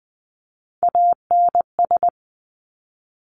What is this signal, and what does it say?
Morse code, heard as a keyed tone, about 20 wpm, 700 Hz. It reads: ANS
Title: Top Three Letter Combinations in Words 20wpm